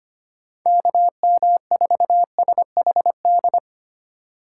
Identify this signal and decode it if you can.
Morse code, heard as a keyed tone, about 25 wpm, 700 Hz. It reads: KM4SHD
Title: USA Call Signs - 2x Character Spacing - 25wpm